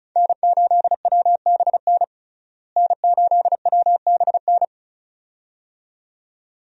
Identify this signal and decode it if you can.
Morse code, heard as a keyed tone, about 34 wpm, 700 Hz. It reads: N8WBN N8WBN
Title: USA Call Signs - 5 Characters - Repeated 2x - 35wpm